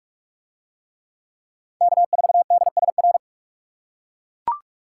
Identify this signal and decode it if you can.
Morse code, heard as a keyed tone, about 40 wpm, 700 Hz. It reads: K4DSR